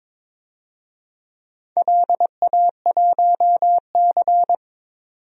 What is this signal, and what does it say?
Morse code, heard as a keyed tone, about 22 wpm, 700 Hz. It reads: LA1C
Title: DXCC-100 Call Signs - 4 Characters - 22wpm